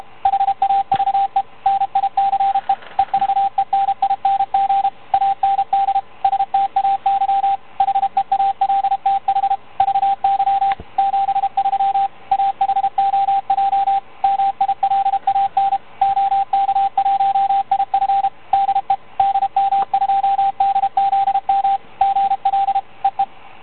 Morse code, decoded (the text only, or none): HAVE NICE EVENING AND STAY HEALTH VY 73 AHOJ MILAN OK1IF DE DM2DZM GL EE